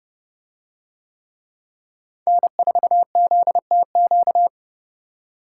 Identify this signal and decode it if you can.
Morse code, heard as a keyed tone, about 29 wpm, 700 Hz. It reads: N4ZTQ